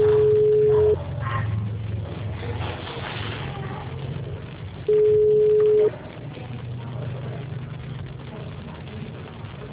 Ringback tone (Europe, 425 Hz).